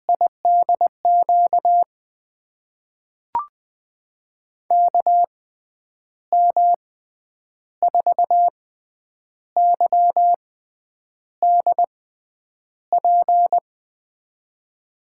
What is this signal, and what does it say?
Morse code, heard as a keyed tone, about 21 wpm, spaced as for 9 wpm, 700 Hz. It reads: IDQ KM4YDP
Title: USA Call Signs - 6 Characters - 6x Character Spacing - 20wpm